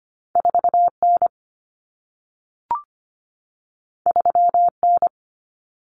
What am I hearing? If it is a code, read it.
Morse code, heard as a keyed tone, about 25 wpm, 700 Hz. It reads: 4N 3N
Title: Common Cut Numbers - 2-Digit 25wpm